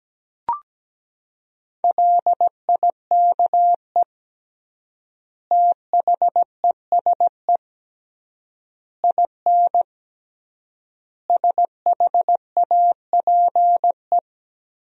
Morse code, heard as a keyed tone, about 17 wpm, 700 Hz. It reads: LIKE THESE IN SHAPE
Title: 3x Word-Spacing - Sentences from Top 500 Words - 17wpm